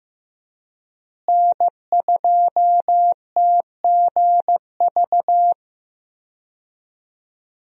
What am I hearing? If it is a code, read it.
Morse code, heard as a keyed tone, about 15 wpm, 700 Hz. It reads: N2TGV
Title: USA Call Signs - 5 Characters - Repeated 3x - 15wpm